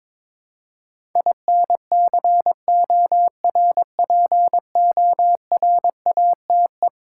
Morse code, heard as a keyed tone, about 22 wpm, 700 Hz. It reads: INCORPORATE
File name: practice / 22wpm / Eleven Letter Words - 22wpm